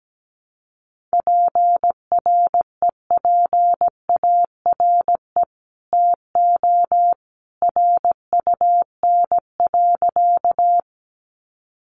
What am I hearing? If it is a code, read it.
Morse code, heard as a keyed tone, about 17 wpm, 700 Hz. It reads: PREPARE TO RUN.
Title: Sentences from Top 900 Words 17wpm